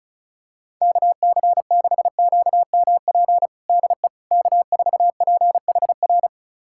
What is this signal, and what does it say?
Morse code, heard as a keyed tone, about 36 wpm, 700 Hz. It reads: KC6QMP DE K4PHR
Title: USA Callsign DE Callsign 35wpm